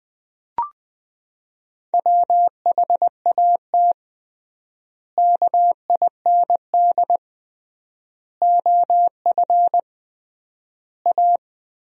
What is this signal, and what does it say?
Morse code, heard as a keyed tone, about 21 wpm, 700 Hz. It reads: WHAT KIND OF A
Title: 3x Word-Spacing - Sentences from Top 400 Words - 20wpm